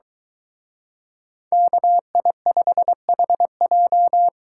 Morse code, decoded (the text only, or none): KI5HJ